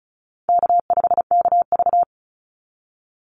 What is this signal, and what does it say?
Morse code, heard as a keyed tone, about 34 wpm, 700 Hz. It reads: K5KV
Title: Easy Call Signs Repeated 3x - 35wpm